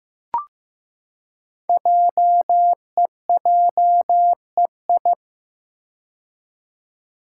Morse code, heard as a keyed tone, about 15 wpm, 700 Hz. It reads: JEJEI
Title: Code Groups - 5 Letter:Numbers - 15wpm